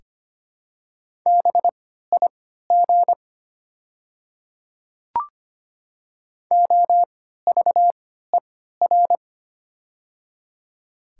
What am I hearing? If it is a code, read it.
Morse code, heard as a keyed tone, about 25 wpm, spaced as for 14 wpm, 700 Hz. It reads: BIG OVER
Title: 3x Character Spacing - Top 200 Words - 25wpm